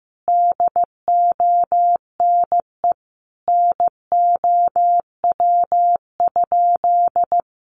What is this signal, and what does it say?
Morse code, heard as a keyed tone, about 15 wpm, 700 Hz. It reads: DONE NOW?
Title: Sentences from Top 1100 Words 15wpm